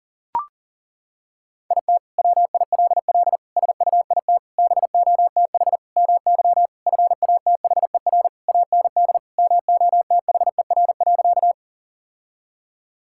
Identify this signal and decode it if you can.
Morse code, heard as a keyed tone, about 40 wpm, 700 Hz. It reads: IT WILL SUIT BOTH MY FATHER AND MOTHER.
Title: Sentences from Top 1600 Words 40wpm